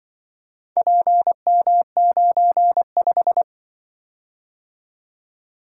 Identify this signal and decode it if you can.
Morse code, heard as a keyed tone, about 24 wpm, 700 Hz. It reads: PM95
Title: Speed-Racing - Four Letter-Number Combinations - 20wpm